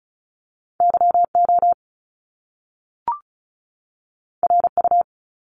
Morse code, heard as a keyed tone, about 36 wpm, 700 Hz. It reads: YO RU
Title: Top Two Letter Combinations in Words 35wpm